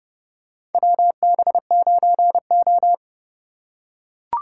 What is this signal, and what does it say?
Morse code, heard as a keyed tone, about 29 wpm, 700 Hz. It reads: WB9O
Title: USA Call Signs - 4 Characters - 4x Character Spacing - 30wpm